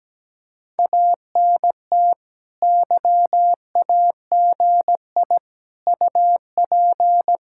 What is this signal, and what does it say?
Morse code, heard as a keyed tone, about 17 wpm, 700 Hz. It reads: ANT YAGI UP
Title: Speed-Racing Common QSO Phrases - 17wpm